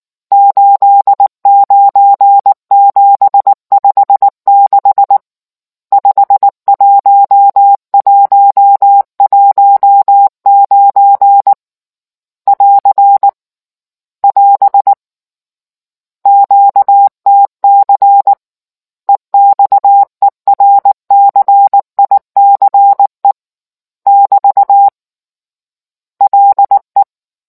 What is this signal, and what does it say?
Morse code, heard as a keyed tone, about 19 wpm, 800 Hz. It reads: 89756 51119 <AR> <AS> QTC EXERCICE <BT> LE